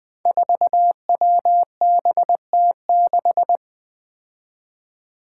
Morse code, heard as a keyed tone, about 20 wpm, 700 Hz. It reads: 4WBT6
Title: Five Challenging Letter-Number Combinations 20wpm